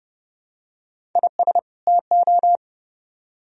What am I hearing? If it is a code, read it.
Morse code, heard as a keyed tone, about 28 wpm, 700 Hz. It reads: IS TO